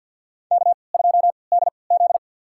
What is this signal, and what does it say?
Morse code, heard as a keyed tone, about 52 wpm, spaced as for 29 wpm, 700 Hz. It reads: K2DZ
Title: USA Call Signs - 4 Characters - 3x Character Spacing - 50wpm